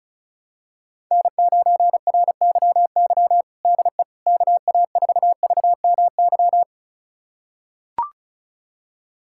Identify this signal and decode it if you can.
Morse code, heard as a keyed tone, about 34 wpm, 700 Hz. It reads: N9RYY DE KA4VMY
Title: USA Callsign DE Callsign 35wpm